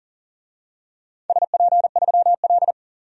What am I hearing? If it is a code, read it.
Morse code, heard as a keyed tone, about 37 wpm, 700 Hz. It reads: SP3L
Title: DXCC-100 Call Signs - 4 Characters - 40wpm